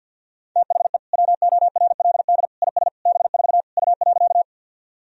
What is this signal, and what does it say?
Morse code, heard as a keyed tone, about 46 wpm, 700 Hz. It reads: THE WORLD IS B4 U.
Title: Sentences from Top 200 Words and Abbreviations 50wpm